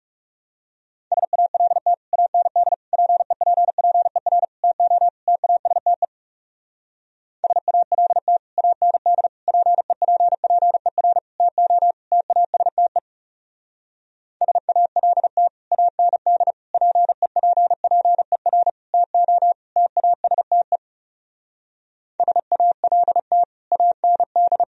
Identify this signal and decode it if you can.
Morse code, heard as a keyed tone, about 40 wpm, 700 Hz. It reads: SALT AND PEPPER TO TASTE SALT AND PEPPER TO TASTE SALT AND PEPPER TO TASTE SALT AND